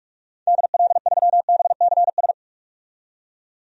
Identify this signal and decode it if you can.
Morse code, heard as a keyed tone, about 44 wpm, 690 Hz. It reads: DL3BCS
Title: DXCC-100 Call Signs - 6 Characters - 45wpm